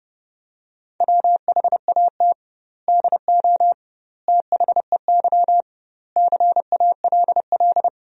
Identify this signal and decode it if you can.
Morse code, heard as a keyed tone, about 30 wpm, 700 Hz. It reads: WHAT DO THEY CALL
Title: 2x Word-Spacing - Sentences from Top 100 Words - 30wpm